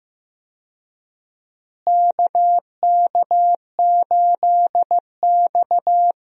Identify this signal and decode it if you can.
Morse code, heard as a keyed tone, about 15 wpm, 700 Hz. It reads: KK8X
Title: USA Call Signs - 4 Characters - Repeated 2x - 15wpm